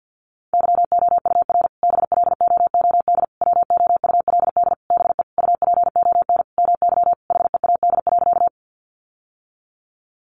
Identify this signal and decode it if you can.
Morse code, heard as a keyed tone, about 50 wpm, 700 Hz. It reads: YOUR BLOOD WOULD BE UPON MY HEAD.
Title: Sentences from Top 1100 Words 50wpm